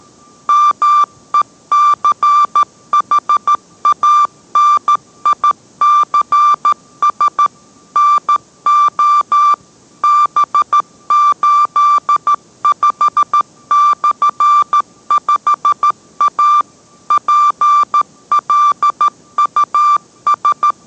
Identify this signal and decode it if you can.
Morse code, heard as a keyed tone, about 13 wpm, 1180 Hz. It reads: MECHANICS NO B85/5A PLUS